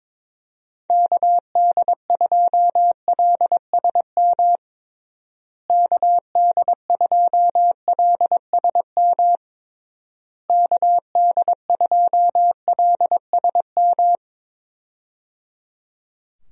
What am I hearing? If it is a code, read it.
Morse code, heard as a keyed tone, about 21 wpm, 700 Hz. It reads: KD2LSM KD2LSM KD2LSM
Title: USA Call Signs - 6 Characters - Repeated 3x - 22wpm